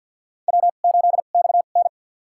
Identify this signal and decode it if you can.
Morse code, heard as a keyed tone, about 53 wpm, spaced as for 37 wpm, 700 Hz. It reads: W8XN
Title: USA Call Signs - 4 Characters - 2x Character Spacing - 50wpm